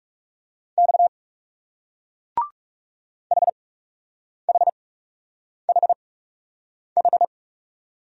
Morse code, heard as a keyed tone, about 38 wpm, spaced as for 8 wpm, 700 Hz. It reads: X HHHH